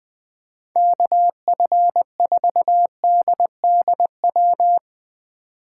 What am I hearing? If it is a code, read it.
Morse code, heard as a keyed tone, about 21 wpm, 700 Hz. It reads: KF4DDW
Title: USA Call Signs - 6 Characters - 2x Character Spacing - 20wpm